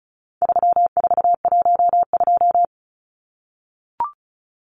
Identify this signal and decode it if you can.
Morse code, heard as a keyed tone, about 34 wpm, 700 Hz. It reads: 3412
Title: Numbers - 4-Digit 35wpm